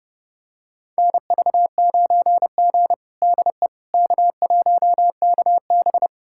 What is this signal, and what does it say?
Morse code, heard as a keyed tone, about 31 wpm, 700 Hz. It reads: NV9G DE K1KB
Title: USA Callsign DE Callsign - 4 Characters - 30wpm